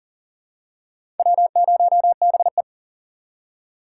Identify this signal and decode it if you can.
Morse code, heard as a keyed tone, about 40 wpm, 700 Hz. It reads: W0BE
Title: USA Call Signs - 4 Characters - 2x Character Spacing - 40wpm